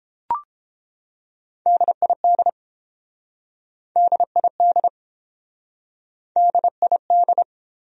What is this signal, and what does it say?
Morse code, heard as a keyed tone, about 30 wpm, 700 Hz. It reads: DID DID DID